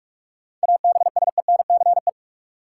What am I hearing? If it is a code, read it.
Morse code, heard as a keyed tone, about 41 wpm, 700 Hz. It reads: ABSENCE
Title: Sets of 4 Words - New Word Drill - 45wpm